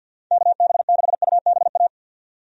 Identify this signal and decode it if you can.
Morse code, heard as a keyed tone, about 50 wpm, 700 Hz. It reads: KB6UBA
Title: USA Call Signs - 6 Characters - 50wpm